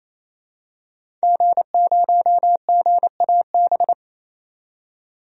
Morse code, heard as a keyed tone, about 28 wpm, 700 Hz. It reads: G0GAB
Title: DXCC-100 Call Signs - 5 Characters - 28wpm